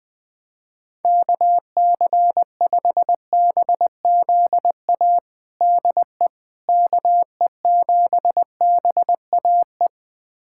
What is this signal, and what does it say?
Morse code, heard as a keyed tone, about 20 wpm, 700 Hz. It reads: KC5BZA DE KE7BAE
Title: USA Callsign DE Callsign 20wpm